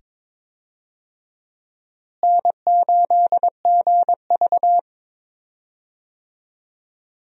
Morse code, heard as a keyed tone, about 22 wpm, 700 Hz. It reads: N8GV
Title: USA Call Signs - 4 Characters - Repeated 2x - 22wpm